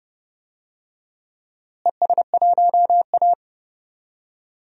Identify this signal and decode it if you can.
Morse code, heard as a keyed tone, about 30 wpm, 700 Hz. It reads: ES1A
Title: DXCC-100 Call Signs - 4 Characters - Repeated 3x - 30wpm